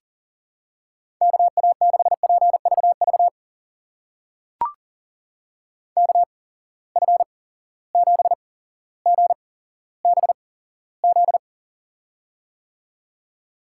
Morse code, heard as a keyed tone, about 40 wpm, spaced as for 17 wpm, 700 Hz. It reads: KA6PVV KF7GBZ
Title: USA Call Signs - 6 Characters - 8x Character Spacing - 40wpm